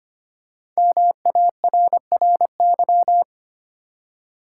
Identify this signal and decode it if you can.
Morse code, heard as a keyed tone, about 25 wpm, 700 Hz. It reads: MARRY
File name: 5 Letter Words - First 3 Letters Given - 25wpm